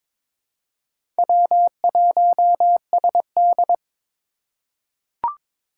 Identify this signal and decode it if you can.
Morse code, heard as a keyed tone, about 22 wpm, 700 Hz. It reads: W1SD
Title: USA Call Signs - 2x Character Spacing - 22wpm